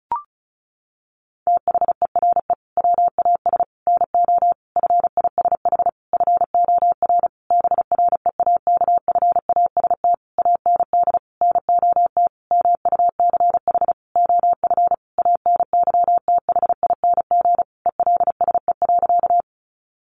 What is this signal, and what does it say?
Morse code, heard as a keyed tone, about 34 wpm, 700 Hz. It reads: THERE WAS NO FISH FOR BREAKFAST AND NOT MUCH OF ANYTHING ELSE.
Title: Sentences from Top 1500 Words 35wpm